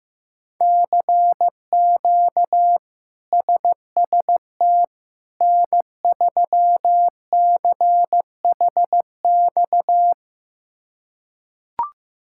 Morse code, heard as a keyed tone, about 15 wpm, 700 Hz. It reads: CQ SST N3CHX
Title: K1USN SST QSO Practice - 15wpm